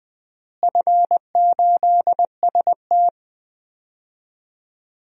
Morse code, heard as a keyed tone, about 19 wpm, 700 Hz. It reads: F8ST
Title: DXCC-100 Call Signs - 4 Characters - Repeated 3x - 20wpm